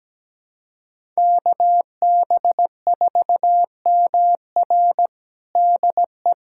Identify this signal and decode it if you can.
Morse code, heard as a keyed tone, about 17 wpm, 700 Hz. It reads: KB4MR DE
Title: USA Callsign DE Callsign - 5 Characters - 17wpm